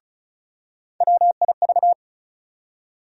Morse code, heard as a keyed tone, about 36 wpm, 700 Hz. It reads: WIV
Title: USA Call Sign Postfixes - 3 Letters 35wpm